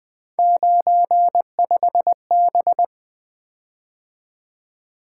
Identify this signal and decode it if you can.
Morse code, heard as a keyed tone, about 20 wpm, 700 Hz. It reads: 95B
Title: Three Letter-Number Combinations - 20wpm